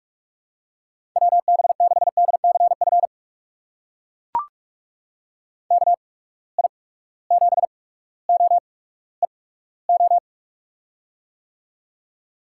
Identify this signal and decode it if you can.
Morse code, heard as a keyed tone, about 43 wpm, spaced as for 19 wpm, 700 Hz. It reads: WB6DCF KI7OEO
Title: USA Call Signs - 6 Characters - 8x Character Spacing - 45wpm